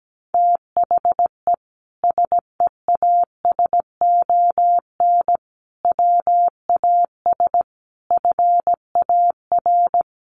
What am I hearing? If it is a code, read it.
Morse code, heard as a keyed tone, about 17 wpm, 700 Hz. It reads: THE SEASON WAS FAR
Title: Sentences from Top 1400 Words 17wpm